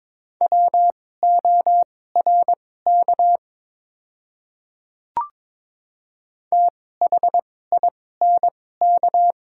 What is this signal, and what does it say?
Morse code, heard as a keyed tone, about 22 wpm, spaced as for 16 wpm, 690 Hz. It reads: WORK THINK
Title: 2x Character Spacing - Top 500 Words - 22wpm